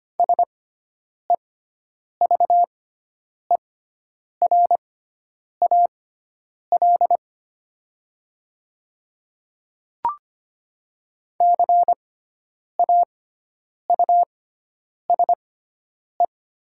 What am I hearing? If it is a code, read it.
Morse code, heard as a keyed tone, about 25 wpm, spaced as for 9 wpm, 700 Hz. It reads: SEVERAL CAUSE
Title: 6x Character Spacing - Top 500 Words - 25wpm